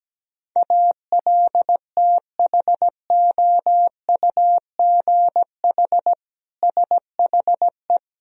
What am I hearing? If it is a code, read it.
Morse code, heard as a keyed tone, about 17 wpm, 700 Hz. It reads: ALTHOUGH SHE
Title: Sentences from Top 1000 Words - Review Part 3 - 17wpm